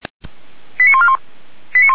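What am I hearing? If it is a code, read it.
Five-tone selective call (ZVEI), heard as quick runs of five tones, tones of about 70 ms. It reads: 89141
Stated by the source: ZVEI five-tone call received on an SDR.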